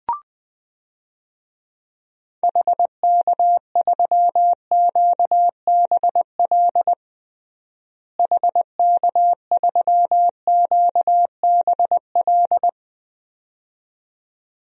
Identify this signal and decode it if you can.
Morse code, heard as a keyed tone, about 20 wpm, 700 Hz. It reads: HK3QBL HK3QBL
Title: DXCC-100 Call Signs - 6 Characters - Repeated 2x - 20wpm